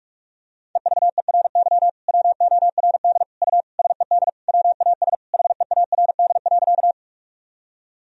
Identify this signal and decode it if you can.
Morse code, heard as a keyed tone, about 46 wpm, 700 Hz. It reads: EVERY WORD U SED WAS HEARD.
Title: Sentences from Top 400 Words and Abbreviations 45wpm